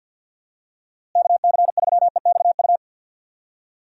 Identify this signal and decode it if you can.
Morse code, heard as a keyed tone, about 52 wpm, 700 Hz. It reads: KC3EXU
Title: USA Call Signs - 6 Characters - Repeated 2x - 50wpm